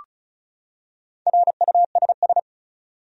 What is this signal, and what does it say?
Morse code, heard as a keyed tone, about 35 wpm, 700 Hz. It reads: RUSS
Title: USA Common Names - Max 4 Letters - 35wpm